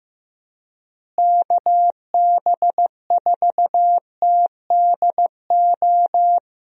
Morse code, heard as a keyed tone, about 15 wpm, 700 Hz. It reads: KB4TDO
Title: USA Call Signs - 6 Characters - Repeated 3x - 15wpm